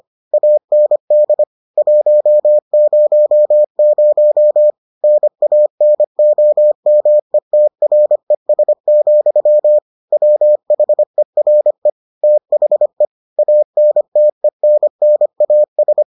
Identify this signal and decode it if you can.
Morse code, heard as a keyed tone, about 25 wpm, 600 Hz. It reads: AND 100 NANOMETRES, WHERE THE ANTENNAS